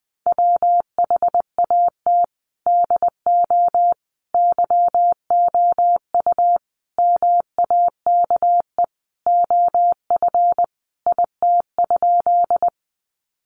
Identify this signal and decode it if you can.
Morse code, heard as a keyed tone, about 20 wpm, 700 Hz. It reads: WHAT DO YOU MAKE OF IT?
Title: Sentences from Top 100 Words 20wpm